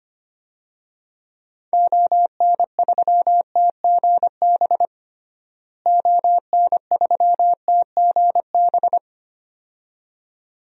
Morse code, heard as a keyed tone, about 25 wpm, 700 Hz. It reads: ON3TGB ON3TGB
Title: DXCC-100 Call Signs - 6 Characters - Repeated 2x - 25wpm